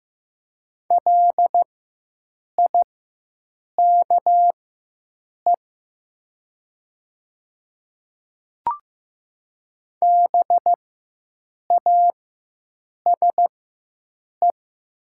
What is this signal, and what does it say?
Morse code, heard as a keyed tone, about 15 wpm, spaced as for 7 wpm, 700 Hz. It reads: LIKE BASE
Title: 4x Character Spacing - Top 300 Words - 15wpm